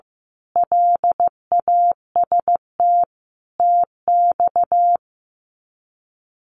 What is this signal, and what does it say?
Morse code, heard as a keyed tone, about 15 wpm, 700 Hz. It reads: LAST TX
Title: Common QSO Phrases 15wpm